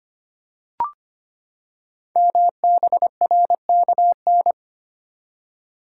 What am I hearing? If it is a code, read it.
Morse code, heard as a keyed tone, about 25 wpm, 700 Hz. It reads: MBRKN
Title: Code Groups - 5 Letters - 25wpm